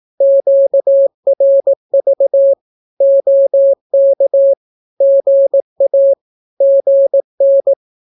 Morse code, heard as a keyed tone, about 18 wpm, 550 Hz. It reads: QRV OK GA GN
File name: naukapodstaw / skroty / skroty